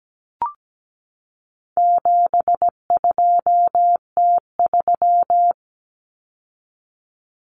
Morse code, heard as a keyed tone, about 17 wpm, 700 Hz. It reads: 72T3
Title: Common Cut Numbers - 4-Digit 17wpm